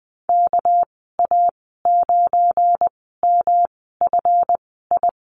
Morse code, heard as a keyed tone, about 20 wpm, spaced as for 15 wpm, 700 Hz. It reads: KA9MFI
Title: 200 Lesson 32g - Letter k - Call Signs 20wpm